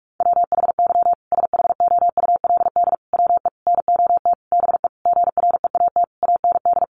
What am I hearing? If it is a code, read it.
Morse code, heard as a keyed tone, about 44 wpm, 700 Hz. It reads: WHY SHOULD WE NOT BE GREAT AND
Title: Sentences from Top 1500 Words 45wpm